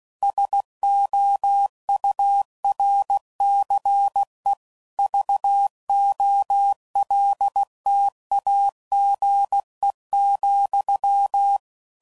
Morse code, heard as a keyed tone, about 16 wpm, 800 Hz. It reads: SOURCE VOLTAGE,
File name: CW Code Groups @15WPM / Technical Subjects / XL